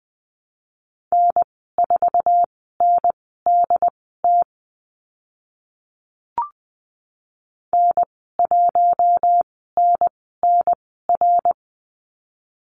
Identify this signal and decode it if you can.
Morse code, heard as a keyed tone, about 20 wpm, spaced as for 14 wpm, 700 Hz. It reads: N4NDT N1NNR
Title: 076 Lesson 13g - Letter L - Call Signs 20wpm